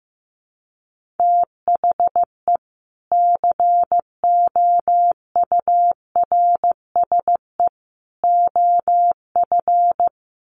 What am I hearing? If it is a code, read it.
Morse code, heard as a keyed tone, about 15 wpm, 700 Hz. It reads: THE COURSE OF
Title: Sets of 3 Words 15wpm